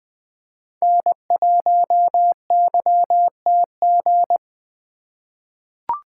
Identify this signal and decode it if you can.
Morse code, heard as a keyed tone, about 20 wpm, 700 Hz. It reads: N1YTG
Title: USA Call Signs - 5 Characters - 4x Character Spacing - 20wpm